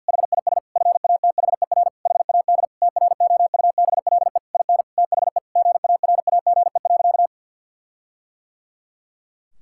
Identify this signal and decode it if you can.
Morse code, heard as a keyed tone, about 50 wpm, 700 Hz. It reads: HIS FATHER HAD TROUBLE IN THE GARAGE.
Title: Sentences from Top 1600 Words 50wpm